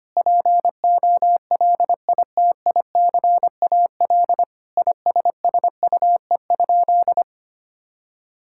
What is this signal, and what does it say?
Morse code, heard as a keyed tone, about 25 wpm, 700 Hz. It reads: POLITICAL ISSUE?
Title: Sentences from Top 1000 Words - Review Part 6 - 25wpm